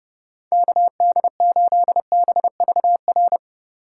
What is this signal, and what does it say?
Morse code, heard as a keyed tone, about 30 wpm, 700 Hz. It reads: KD8BVR